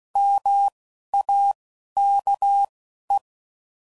Morse code, heard as a keyed tone, about 16 wpm, spaced as for 11 wpm, 800 Hz. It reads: MAKE